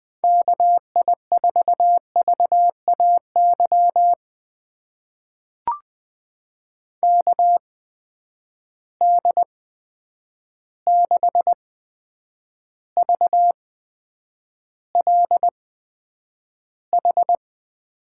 Morse code, heard as a keyed tone, about 20 wpm, spaced as for 9 wpm, 700 Hz. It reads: KI4VAY KD6VLH
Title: USA Call Signs - 6 Characters - 8x Character Spacing - 20wpm